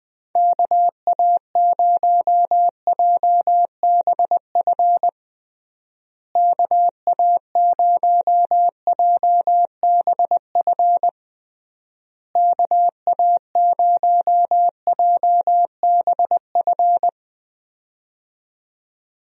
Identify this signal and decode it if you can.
Morse code, heard as a keyed tone, about 20 wpm, 700 Hz. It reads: KA0JBF KA0JBF KA0JBF
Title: USA Call Signs - 6 Characters - Repeated 3x - 20wpm